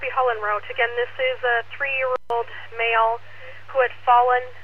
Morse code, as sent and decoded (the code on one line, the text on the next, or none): none